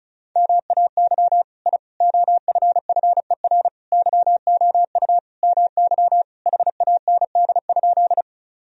Morse code, heard as a keyed tone, about 34 wpm, 700 Hz. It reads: MAY I OFFER YOU MY HAND?
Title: Sentences from Top 1000 Words - Review Part 4 - 35wpm